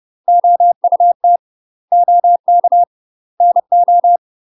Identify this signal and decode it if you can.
Morse code, heard as a keyed tone, about 30 wpm, 700 Hz. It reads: OUT OK NO
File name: UK MASTER QSOs TRAINING / 30WPM / UK MASTER QSOs 0018